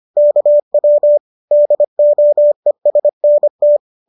Morse code, heard as a keyed tone, about 25 wpm, 590 Hz. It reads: KW DOESNT
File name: qso221 25